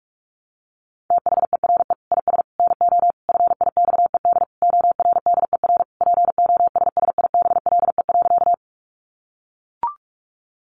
Morse code, heard as a keyed tone, about 44 wpm, 700 Hz. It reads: THERE IS NO FIXED ORDER POSSIBLE.
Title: Sentences from Top 1100 Words 45wpm